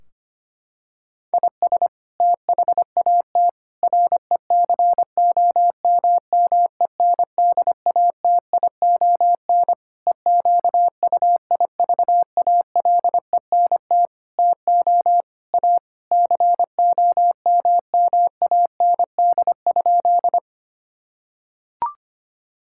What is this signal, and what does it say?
Morse code, heard as a keyed tone, about 25 wpm, 700 Hz. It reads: IS THAT RECOMMENDATION EQUIVALENT TO A COMMAND?
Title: Sentences from Top 1600 Words 25wpm